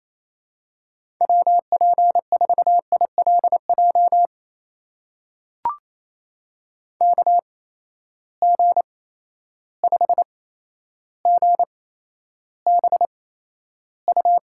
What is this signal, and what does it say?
Morse code, heard as a keyed tone, about 28 wpm, spaced as for 12 wpm, 700 Hz. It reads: WP4ILJ KG5GBU
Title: USA Call Signs - 6 Characters - 8x Character Spacing - 28wpm